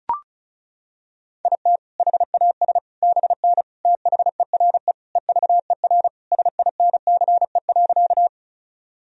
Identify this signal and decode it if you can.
Morse code, heard as a keyed tone, about 36 wpm, 700 Hz. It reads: IT HAS BN THERE EVER SINCE.
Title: Sentences from Top 400 Words and Abbreviations 35wpm